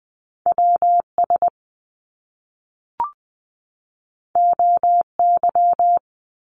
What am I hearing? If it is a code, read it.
Morse code, heard as a keyed tone, about 19 wpm, 700 Hz. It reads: WS OY
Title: Top Two Letter Combinations in Words 20wpm